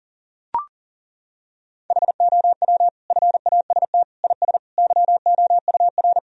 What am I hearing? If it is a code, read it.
Morse code, heard as a keyed tone, about 37 wpm, 700 Hz. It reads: HOW FAST IS YOUR
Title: Sentences from Top 1000 Words - Review Part 4 - 40wpm